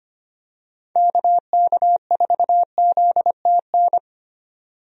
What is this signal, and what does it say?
Morse code, heard as a keyed tone, about 25 wpm, 700 Hz. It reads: KK4ZTN
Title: USA Call Signs - 6 Characters - 25wpm